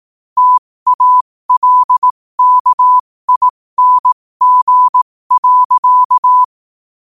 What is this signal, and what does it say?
Morse code, heard as a keyed tone, about 18 wpm, spaced as for 15 wpm, 1000 Hz. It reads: TALKING.
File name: qotdfiles / 15wpm